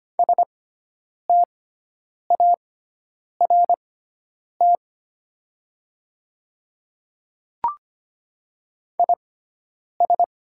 Morse code, heard as a keyed tone, about 25 wpm, spaced as for 9 wpm, 700 Hz. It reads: START IS
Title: 6x Character Spacing - Top 300 Words - 25wpm